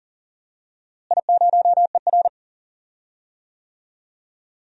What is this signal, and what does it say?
Morse code, heard as a keyed tone, about 40 wpm, 700 Hz. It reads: I0ER
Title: DXCC-100 Call Signs - 4 Characters - Repeated 2x - 40wpm